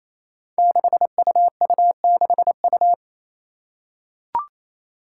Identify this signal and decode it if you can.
Morse code, heard as a keyed tone, about 28 wpm, 700 Hz. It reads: 6UU6U
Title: Five Challenging Letter-Number Combinations 28wpm